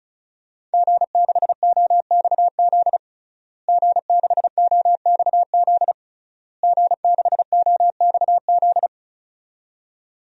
Morse code, heard as a keyed tone, about 36 wpm, 700 Hz. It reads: G6OXZ G6OXZ G6OXZ
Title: DXCC-100 Call Signs - 5 Characters - Repeated 3x - 35wpm